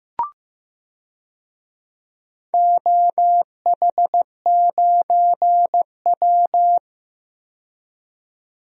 Morse code, heard as a keyed tone, about 15 wpm, 700 Hz. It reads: OH9W
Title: DXCC-100 Call Signs - 4 Characters - 15wpm